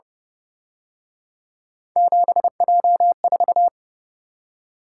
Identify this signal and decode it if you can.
Morse code, heard as a keyed tone, about 30 wpm, 700 Hz. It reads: 7J4